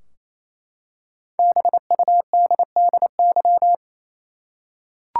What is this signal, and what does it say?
Morse code, heard as a keyed tone, about 28 wpm, 700 Hz. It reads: BUDDY
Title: Five Letter Words - 28wpm